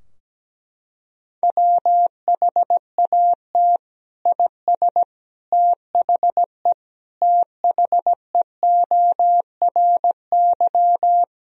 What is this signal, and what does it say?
Morse code, heard as a keyed tone, about 17 wpm, 700 Hz. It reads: WHAT IS THE THEORY
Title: Sentences from Top 1000 Words - Review Part 3 - 17wpm